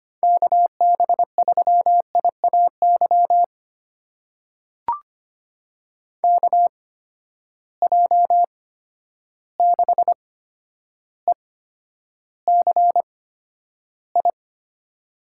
Morse code, heard as a keyed tone, about 25 wpm, spaced as for 11 wpm, 700 Hz. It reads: KB3IAY KJ6ECI
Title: USA Call Signs - 6 Characters - 8x Character Spacing - 25wpm